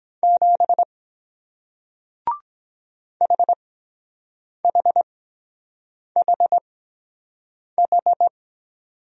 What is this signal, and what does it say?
Morse code, heard as a keyed tone, about 23 wpm, spaced as for 7 wpm, 700 Hz. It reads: 7 HHHH